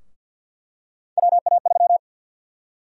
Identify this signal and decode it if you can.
Morse code, heard as a keyed tone, about 51 wpm, 700 Hz. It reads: WA3
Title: USA Call Sign Prefixes - 50wpm